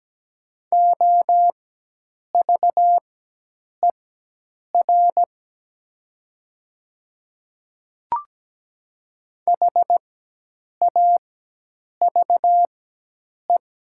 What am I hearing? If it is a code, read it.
Morse code, heard as a keyed tone, about 17 wpm, spaced as for 8 wpm, 700 Hz. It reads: OVER HAVE
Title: 4x Character Spacing - Top 100 Words - 17wpm